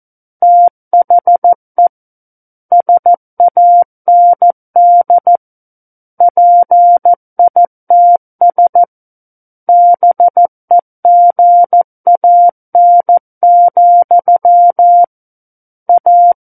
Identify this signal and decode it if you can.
Morse code, heard as a keyed tone, about 14 wpm, 700 Hz. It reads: THE SAND PITS BEGAN, A
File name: Book One / 05-Chapter 5 - 14 WPM